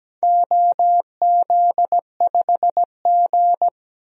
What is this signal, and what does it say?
Morse code, heard as a keyed tone, about 17 wpm, 700 Hz. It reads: OZ5G
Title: DXCC-100 Call Signs - 4 Characters - Repeated 3x - 17wpm